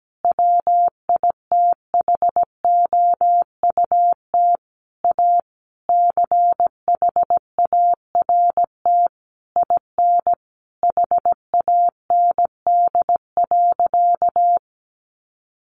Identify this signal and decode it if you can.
Morse code, heard as a keyed tone, about 17 wpm, 700 Hz. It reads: WITHOUT A CHART IN HAND.
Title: Sentences from Top 1300 Words 17wpm